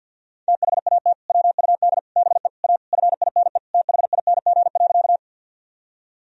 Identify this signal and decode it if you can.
Morse code, heard as a keyed tone, about 49 wpm, 700 Hz. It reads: THAT WUD BE A FINE THING.
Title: Sentences from Top 500 Words and Abbreviations 50wpm